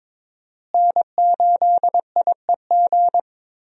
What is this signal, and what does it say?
Morse code, heard as a keyed tone, about 22 wpm, 700 Hz. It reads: N8IEG
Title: USA Call Signs - 5 Characters - Repeated 2x - 22wpm